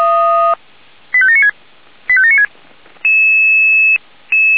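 Five-tone selective call (ZVEI), heard as quick runs of five tones, tones of about 70 ms. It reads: 85796 85796
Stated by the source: ZVEI five-tone selective call received on SDR.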